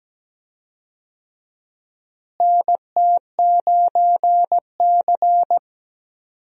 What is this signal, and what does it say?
Morse code, heard as a keyed tone, about 17 wpm, 700 Hz. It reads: NT9C
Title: USA Call Signs - 4 Characters - Repeated 2x - 17wpm